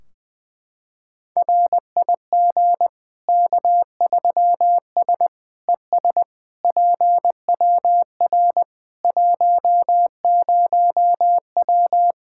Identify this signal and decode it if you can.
Morse code, heard as a keyed tone, about 20 wpm, 700 Hz. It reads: RIG K3S ES PWR 10W
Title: Practice QSO - Exchange 2 - Info - 20wpm